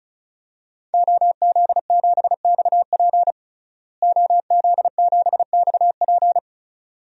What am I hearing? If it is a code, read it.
Morse code, heard as a keyed tone, about 36 wpm, 700 Hz. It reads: OZ7XP OZ7XP
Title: DXCC-100 Call Signs - 5 Characters - Repeated 2x - 35wpm